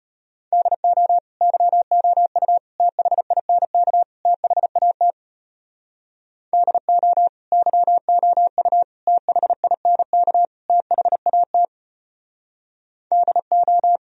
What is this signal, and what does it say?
Morse code, heard as a keyed tone, about 33 wpm, 700 Hz. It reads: DO YOU THINK THAT DO YOU THINK THAT DO